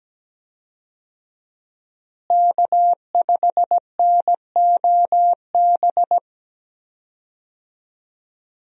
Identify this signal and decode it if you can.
Morse code, heard as a keyed tone, about 17 wpm, 700 Hz. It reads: K5NOB
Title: USA Call Signs - 5 Characters - Repeated 3x - 17wpm